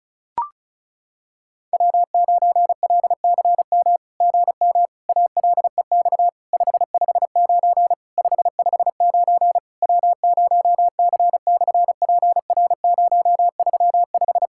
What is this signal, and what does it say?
Morse code, heard as a keyed tone, about 36 wpm, 690 Hz. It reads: W9LCM GM ALEX 559 559 W0C/PR035
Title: SOTA QSO Practice 35wpm